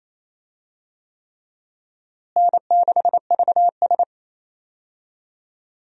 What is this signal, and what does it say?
Morse code, heard as a keyed tone, about 28 wpm, 700 Hz. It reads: N6VS
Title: USA Call Signs - 4 Characters - Repeated 3x - 28wpm